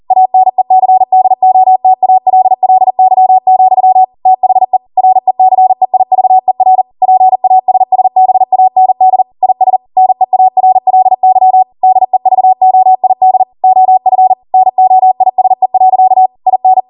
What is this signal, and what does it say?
Morse code, heard as a keyed tone, about 40 wpm, 750 Hz. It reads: ANECDOTALLY, THE RECEIVER PASSBAND IS NEARLY DEVOID OF NOISE. IN